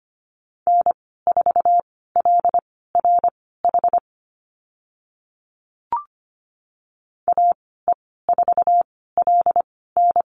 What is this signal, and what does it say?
Morse code, heard as a keyed tone, about 25 wpm, spaced as for 16 wpm, 700 Hz. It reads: N4LRH AE4LN
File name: Lesson 15g - Number 5 - Call Signs 25wpm